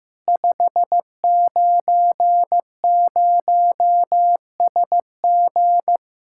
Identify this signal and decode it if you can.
Morse code, heard as a keyed tone, about 16 wpm, 700 Hz. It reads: 590SG
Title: Practice QSO - Exchange 2 - Info - 15wpm